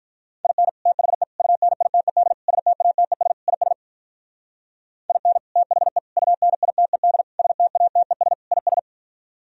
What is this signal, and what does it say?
Morse code, heard as a keyed tone, about 44 wpm, 700 Hz. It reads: IN THE UNITED STATES IS IN THE UNITED STATES IS